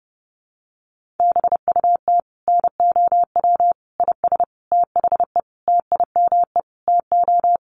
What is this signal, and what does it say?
Morse code, heard as a keyed tone, about 30 wpm, 700 Hz. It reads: BUT NOW IS THE TIME TO
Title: Sentences from Top 1500 Words 30wpm